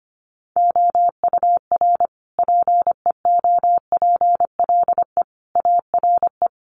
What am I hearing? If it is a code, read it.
Morse code, heard as a keyed tone, about 25 wpm, 700 Hz. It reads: OUR PEOPLE ARE
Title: Sentences from Top 1500 Words 25wpm